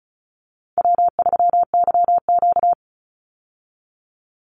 Morse code, heard as a keyed tone, about 34 wpm, 700 Hz. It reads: W3YQ
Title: Easy Call Signs Repeated 2x - 35wpm